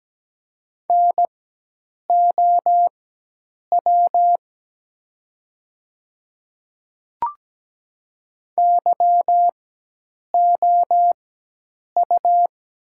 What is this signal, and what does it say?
Morse code, heard as a keyed tone, about 17 wpm, spaced as for 8 wpm, 700 Hz. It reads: NOW YOU
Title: 4x Character Spacing - Top 200 Words - 17wpm